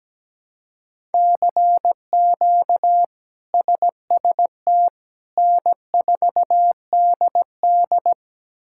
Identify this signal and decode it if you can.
Morse code, heard as a keyed tone, about 17 wpm, 700 Hz. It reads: CQ SST N4DD
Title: K1USN SST QSO Practice - 17wpm